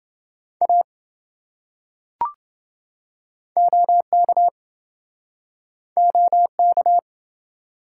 Morse code, heard as a keyed tone, about 29 wpm, 700 Hz. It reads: A OK OK